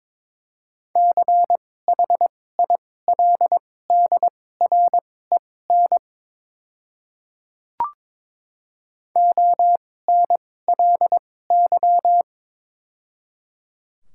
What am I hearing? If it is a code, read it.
Morse code, heard as a keyed tone, about 22 wpm, spaced as for 16 wpm, 700 Hz. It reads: CHILDREN ONLY
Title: 2x Character Spacing - Top 300 Words - 22wpm